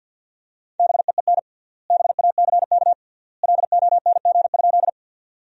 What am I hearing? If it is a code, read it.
Morse code, heard as a keyed tone, about 52 wpm, 700 Hz. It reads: BEEN BACK LONG?
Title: 3x Word-Spacing - Sentences from Top 200 Words - 50wpm